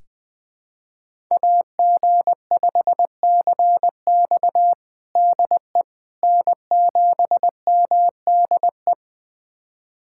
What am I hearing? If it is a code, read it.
Morse code, heard as a keyed tone, about 20 wpm, 700 Hz. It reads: AG5CX DE N7MDE
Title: USA Callsign DE Callsign - 5 Characters - 20wpm